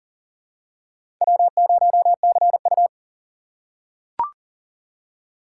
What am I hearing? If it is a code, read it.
Morse code, heard as a keyed tone, about 40 wpm, 700 Hz. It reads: W0CU
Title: USA Call Signs - 4 Characters - 3x Character Spacing - 40wpm